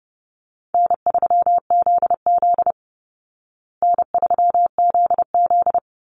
Morse code, heard as a keyed tone, about 30 wpm, 700 Hz. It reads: N3ZZ N3ZZ
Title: Call Signs Repeated 3x - 30wpm